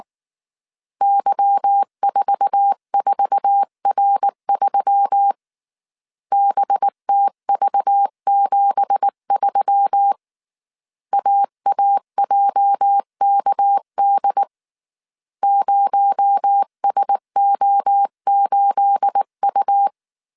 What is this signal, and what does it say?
Morse code, heard as a keyed tone, about 19 wpm, 800 Hz. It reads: Y44R3 BTV73 AAJKD 0SO8U
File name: CW / DET / 1 9 1500C S B 18wpm 2551